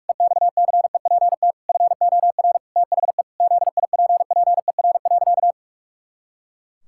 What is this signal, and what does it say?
Morse code, heard as a keyed tone, about 44 wpm, 700 Hz. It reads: EXCEPT FOR THE ZIPPER.
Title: Pangram Challenge v2 - 45wpm